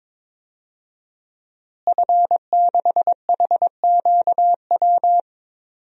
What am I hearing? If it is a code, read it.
Morse code, heard as a keyed tone, about 23 wpm, 700 Hz. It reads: F6HQW
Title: DXCC-100 Call Signs - 5 Characters - Repeated 2x - 22wpm